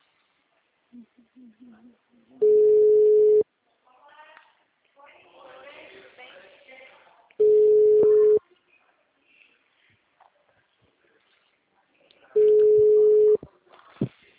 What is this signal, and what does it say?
Ringback tone (Europe, 425 Hz), with speech.
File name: out- -2003- - - .402